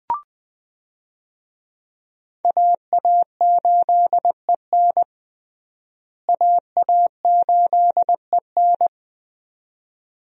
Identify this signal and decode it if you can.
Morse code, heard as a keyed tone, about 20 wpm, 700 Hz. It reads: AA8EN AA8EN
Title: USA Call Signs - 5 Characters - Repeated 2x - 20wpm